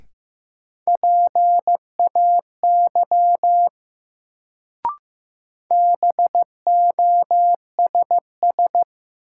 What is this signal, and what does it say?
Morse code, heard as a keyed tone, about 15 wpm, 700 Hz. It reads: PAY BOSS
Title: Sentences from Top 700 Words - New Word Drill - 15wpm